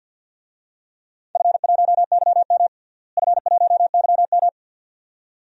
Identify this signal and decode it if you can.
Morse code, heard as a keyed tone, about 45 wpm, 700 Hz. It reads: F1YM F1YM
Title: DXCC-100 Call Signs - 4 Characters - Repeated 2x - 50wpm